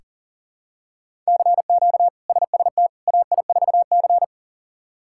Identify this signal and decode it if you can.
Morse code, heard as a keyed tone, about 40 wpm, 690 Hz. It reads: CQ SST AI4C
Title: K1USN SST QSO Practice - 40wpm